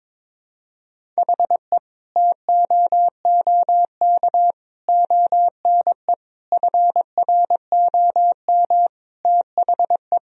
Morse code, heard as a keyed tone, about 22 wpm, 700 Hz. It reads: HE TOOK ONE FROM THE
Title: Sentences from Top 1000 Words - Review Part 1 - 22wpm